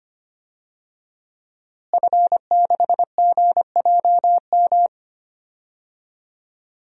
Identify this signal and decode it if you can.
Morse code, heard as a keyed tone, about 25 wpm, 700 Hz. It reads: F6GJM
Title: DXCC-100 Call Signs - 5 Characters - Repeated 3x - 25wpm